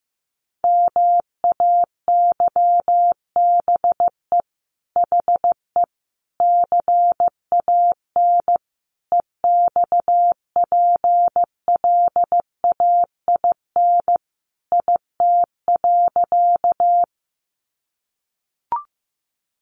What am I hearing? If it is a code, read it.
Morse code, heard as a keyed tone, about 15 wpm, 700 Hz. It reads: MAYBE HE CAN EXPLAIN IT.
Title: Sentences from Top 1100 Words 15wpm